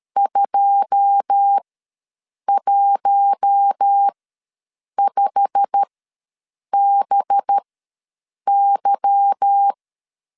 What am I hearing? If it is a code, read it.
Morse code, heard as a keyed tone, about 13 wpm, spaced as for 7 wpm, 800 Hz. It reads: 215BY
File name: LEZIONI / 9 2 ws 12wpm 3406